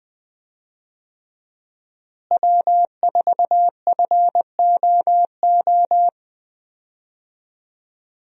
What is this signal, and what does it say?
Morse code, heard as a keyed tone, about 20 wpm, 700 Hz. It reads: W4FOO
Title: USA Call Signs - 5 Characters - Repeated 3x - 20wpm